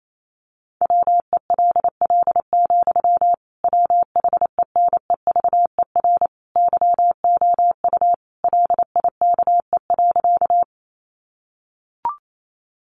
Morse code, heard as a keyed tone, about 28 wpm, 700 Hz. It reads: WELL, WHENEVER YOU LIKE.
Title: Sentences from Top 1200 Words 28wpm